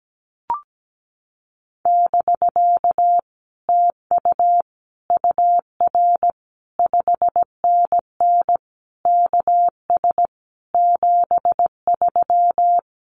Morse code, heard as a keyed tone, about 17 wpm, 700 Hz. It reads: <BK> TU UR 5NN KS 73
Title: Common QSO Phrases Encore 17wpm